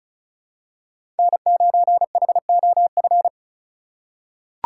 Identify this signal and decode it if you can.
Morse code, heard as a keyed tone, about 36 wpm, 700 Hz. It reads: N9HOF
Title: USA Call Signs - 5 Characters - 3x Character Spacing - 35wpm